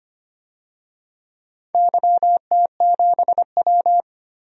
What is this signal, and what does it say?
Morse code, heard as a keyed tone, about 25 wpm, 700 Hz. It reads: YT7W
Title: DXCC-100 Call Signs - 4 Characters - 25wpm